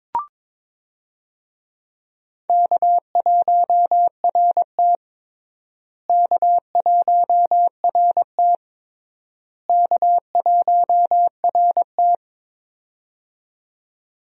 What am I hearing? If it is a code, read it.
Morse code, heard as a keyed tone, about 22 wpm, 700 Hz. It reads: K1RT K1RT K1RT
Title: USA Call Signs - 4 Characters - Repeated 3x - 22wpm